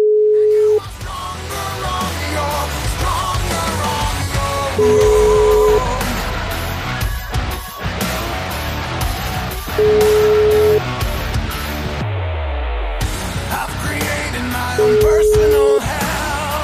Ringback tone (Europe, 425 Hz), with noise.